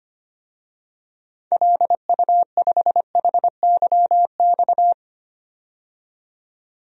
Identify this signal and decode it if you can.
Morse code, heard as a keyed tone, about 25 wpm, 700 Hz. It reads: LU5HYX
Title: DXCC-100 Call Signs - 6 Characters - 25wpm